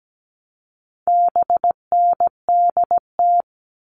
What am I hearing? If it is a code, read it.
Morse code, heard as a keyed tone, about 17 wpm, 700 Hz. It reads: BNDT
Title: All Cut Numbers - 4-Digit 17wpm